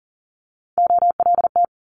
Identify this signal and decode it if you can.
Morse code, heard as a keyed tone, about 42 wpm, 700 Hz. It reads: OLT
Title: Multi-letter Phonemes Quick Fire 40wpm